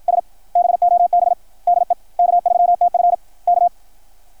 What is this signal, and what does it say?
Morse code, heard as a keyed tone, about 53 wpm, 660 Hz, with radio noise. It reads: S BOB DE K3TF K
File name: archived / CRSnet